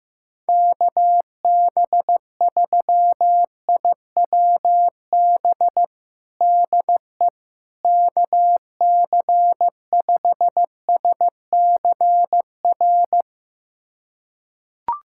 Morse code, heard as a keyed tone, about 16 wpm, 700 Hz. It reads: KB3IWB DE KC5SCR
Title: USA Callsign DE Callsign - 6 Characters - 15wpm